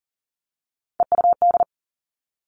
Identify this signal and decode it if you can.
Morse code, heard as a keyed tone, about 38 wpm, 700 Hz. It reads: EUD
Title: Three Challenging Letter-Number Combinations 40wpm